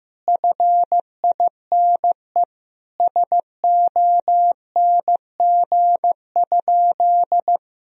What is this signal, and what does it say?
Morse code, heard as a keyed tone, about 15 wpm, 700 Hz. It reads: FINE SONG?
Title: Sentences from Top 1000 Words - Review Part 2 - 15wpm